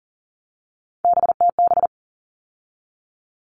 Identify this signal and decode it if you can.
Morse code, heard as a keyed tone, about 40 wpm, 700 Hz. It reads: BTB
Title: All Cut Numbers - 3-Digit 40wpm